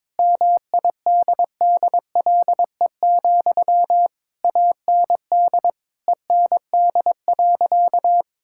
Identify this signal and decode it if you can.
Morse code, heard as a keyed tone, about 22 wpm, 700 Hz. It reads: MIDDLE, AND END.
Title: Sentences from Top 1000 Words - Review Part 6 - 22wpm